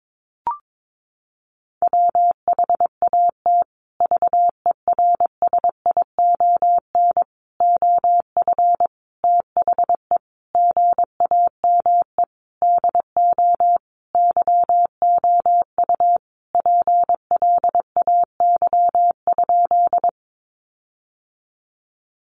Morse code, heard as a keyed tone, about 22 wpm, 700 Hz. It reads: WHAT VERSION OF THE GAME DO YOU PLAY?
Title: Sentences from Top 1400 Words 22wpm